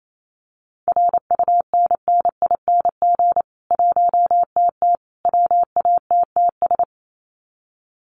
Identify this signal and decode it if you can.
Morse code, heard as a keyed tone, about 28 wpm, 700 Hz. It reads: RUNNING 1TT WATTS
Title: Common QSO Phrases 28wpm